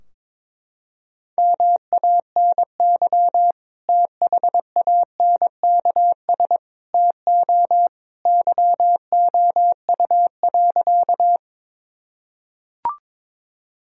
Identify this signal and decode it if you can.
Morse code, heard as a keyed tone, about 21 wpm, 700 Hz. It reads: MANY THANKS TO YOU.
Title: Sentences from Top 1000 Words - Review Part 3 - 22wpm